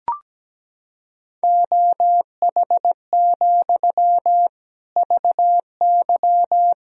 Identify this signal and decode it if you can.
Morse code, heard as a keyed tone, about 17 wpm, 700 Hz. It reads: OH, VY
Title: Sentences from Top 500 Words and Abbreviations 17wpm